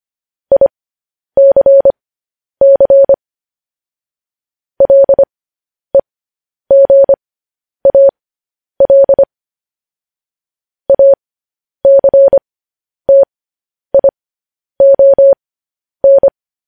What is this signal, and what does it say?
Morse code, heard as a keyed tone, about 25 wpm, spaced as for 10 wpm, 550 Hz. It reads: ICC LEGAL ACTION